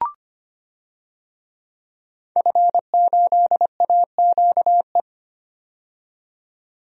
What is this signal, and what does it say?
Morse code, heard as a keyed tone, about 25 wpm, 700 Hz. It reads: F8AQE
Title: DXCC-100 Call Signs - 5 Characters - 25wpm